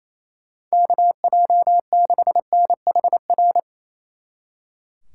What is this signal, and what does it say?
Morse code, heard as a keyed tone, about 34 wpm, 700 Hz. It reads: KJ6NHR
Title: Speed Racing - USA 2x3 Call Signs 28wpm